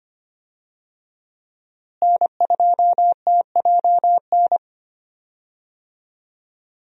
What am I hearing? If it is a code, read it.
Morse code, heard as a keyed tone, about 25 wpm, 700 Hz. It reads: N2TJN
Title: USA Call Signs - 5 Characters - Repeated 3x - 25wpm